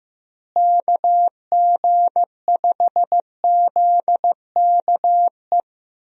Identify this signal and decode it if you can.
Morse code, heard as a keyed tone, about 15 wpm, 700 Hz. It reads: KG5ZKE
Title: USA Call Signs - 6 Characters - Repeated 2x - 15wpm